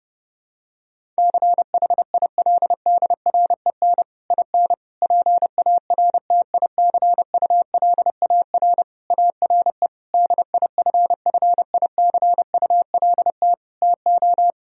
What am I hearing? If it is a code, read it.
Morse code, heard as a keyed tone, about 30 wpm, 700 Hz. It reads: CHILDREN IN PARTICULAR ARE DIFFICULT TO
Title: Sentences from Top 1000 Words - Review Part 4 - 30wpm